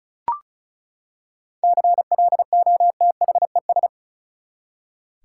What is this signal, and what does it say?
Morse code, heard as a keyed tone, about 36 wpm, 700 Hz. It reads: CLOTHES
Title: Seven Letter Words - 35wpm